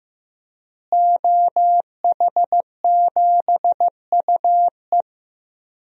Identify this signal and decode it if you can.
Morse code, heard as a keyed tone, about 15 wpm, 700 Hz. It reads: OH7UE
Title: DXCC-100 Call Signs - 5 Characters - Repeated 3x - 15wpm